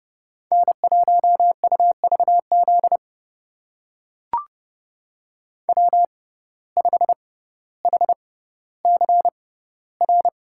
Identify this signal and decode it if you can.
Morse code, heard as a keyed tone, about 30 wpm, spaced as for 15 wpm, 700 Hz. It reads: N1UVZ W5HCR
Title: USA Call Signs - 5 Characters - 6x Character Spacing - 30wpm